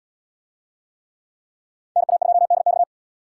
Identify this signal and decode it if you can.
Morse code, heard as a keyed tone, about 69 wpm, 700 Hz. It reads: NA2DX